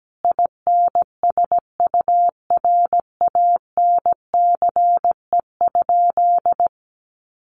Morse code, heard as a keyed tone, about 17 wpm, 700 Hz. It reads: INSURANCE?
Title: Sentences from Top 1100 Words 17wpm